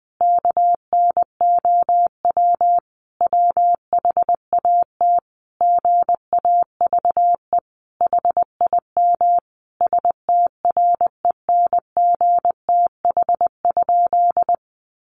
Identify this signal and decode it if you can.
Morse code, heard as a keyed tone, about 20 wpm, 700 Hz. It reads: KNOW WHAT GAVE HIM STRENGTH?
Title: Sentences from Top 1300 Words 20wpm